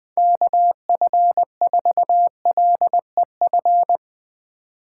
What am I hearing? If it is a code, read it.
Morse code, heard as a keyed tone, about 20 wpm, 700 Hz. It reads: KF4LEF
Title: Speed Racing - USA 2x3 Call Signs 17wpm